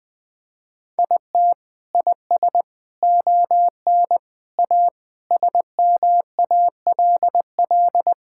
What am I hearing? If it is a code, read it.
Morse code, heard as a keyed tone, about 21 wpm, 700 Hz. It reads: IT IS ON A SMALL
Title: Sentences from Top 1000 Words - Review Part 5 - 20wpm